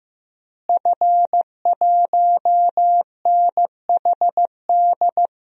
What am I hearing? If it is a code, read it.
Morse code, heard as a keyed tone, about 15 wpm, 700 Hz. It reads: F1NHD
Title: DXCC-100 Call Signs - 5 Characters - Repeated 2x - 15wpm